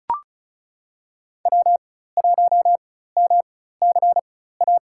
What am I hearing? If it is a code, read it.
Morse code, heard as a keyed tone, about 34 wpm, spaced as for 16 wpm, 700 Hz. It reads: W1MCA
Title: USA Call Signs - 5 Characters - 4x Character Spacing - 35wpm